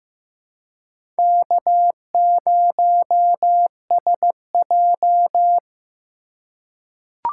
Morse code, heard as a keyed tone, about 15 wpm, 700 Hz. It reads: K0SJ
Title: USA Call Signs - 4 Characters - 3x Character Spacing - 15wpm